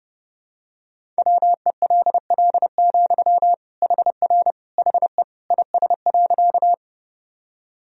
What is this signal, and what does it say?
Morse code, heard as a keyed tone, about 30 wpm, 700 Hz. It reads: WELL, HR HE IS.
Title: Sentences from Top 200 Words and Abbreviations 30wpm